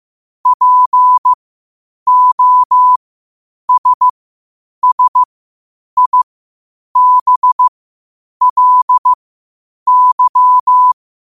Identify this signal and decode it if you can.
Morse code, heard as a keyed tone, about 15 wpm, spaced as for 8 wpm, 1000 Hz. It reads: POSSIBLY
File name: qotdfiles / 8wpm